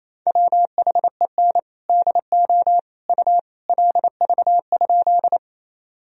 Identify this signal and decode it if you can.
Morse code, heard as a keyed tone, about 28 wpm, 700 Hz. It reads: WHEN DO U LV?
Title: Sentences from Top 400 Words and Abbreviations 28wpm